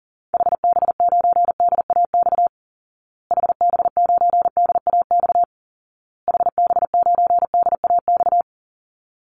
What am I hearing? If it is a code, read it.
Morse code, heard as a keyed tone, about 40 wpm, 700 Hz. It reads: HB9DAX HB9DAX HB9DAX
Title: Call Signs Repeated 2x - 40wpm